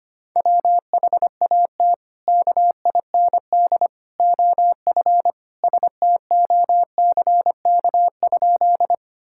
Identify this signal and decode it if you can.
Morse code, heard as a keyed tone, about 25 wpm, 700 Hz. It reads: WHAT KIND OF STOCK?
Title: Sentences from Top 1000 Words - Review Part 5 - 25wpm